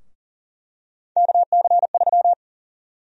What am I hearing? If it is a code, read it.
Morse code, heard as a keyed tone, about 40 wpm, 700 Hz. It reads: KC3
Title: USA Call Sign Prefixes - Repeated 2x - 40wpm